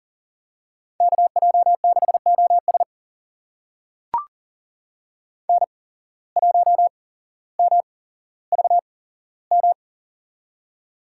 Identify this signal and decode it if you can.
Morse code, heard as a keyed tone, about 40 wpm, spaced as for 17 wpm, 700 Hz. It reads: KJ6OS N1MVM
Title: USA Call Signs - 5 Characters - 8x Character Spacing - 40wpm